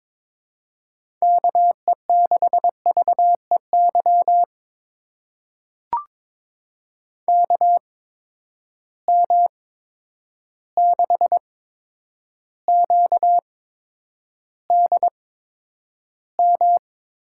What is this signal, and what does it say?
Morse code, heard as a keyed tone, about 22 wpm, spaced as for 9 wpm, 700 Hz. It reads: KE6VEY KM6QDM
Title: USA Call Signs - 6 Characters - 8x Character Spacing - 22wpm